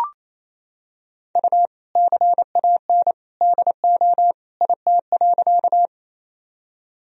Morse code, heard as a keyed tone, about 28 wpm, 700 Hz. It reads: U CAN DO IT.
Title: Sentences from Top 100 Words and Abbreviations 28wpm